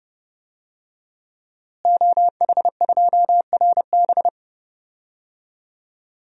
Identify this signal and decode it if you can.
Morse code, heard as a keyed tone, about 30 wpm, 700 Hz. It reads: OH2RB
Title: DXCC-100 Call Signs - 5 Characters - Repeated 2x - 30wpm